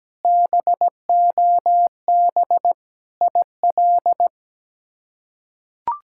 Morse code, heard as a keyed tone, about 17 wpm, 700 Hz. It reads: BOB IL
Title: K1USN SST QSO Practice - 17wpm